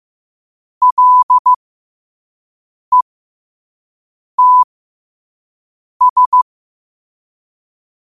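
Morse code, heard as a keyed tone, about 15 wpm, spaced as for 5 wpm, 1000 Hz. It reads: LETS
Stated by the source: Morse code sent at 5 wpm.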